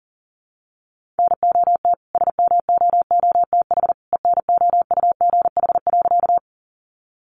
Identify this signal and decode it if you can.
Morse code, heard as a keyed tone, about 40 wpm, 700 Hz. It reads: NOT SMOOTH ENOUGH.
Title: Sentences from Top 1400 Words 40wpm